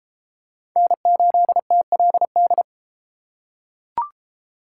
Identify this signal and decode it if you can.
Morse code, heard as a keyed tone, about 29 wpm, 700 Hz. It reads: N8TLD